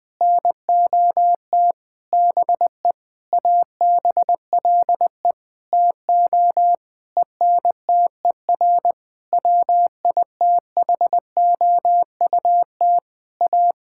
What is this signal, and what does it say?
Morse code, heard as a keyed tone, about 20 wpm, 700 Hz. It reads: NOT BE ABLE TO ENTER WITHOUT A
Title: Sentences with 6-Word Phrases - Spoken Beginning - 20wpm